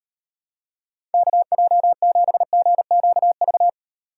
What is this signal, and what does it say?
Morse code, heard as a keyed tone, about 38 wpm, 700 Hz. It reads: KJ7GQV